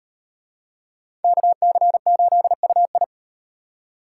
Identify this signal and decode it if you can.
Morse code, heard as a keyed tone, about 38 wpm, 700 Hz. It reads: KC8UI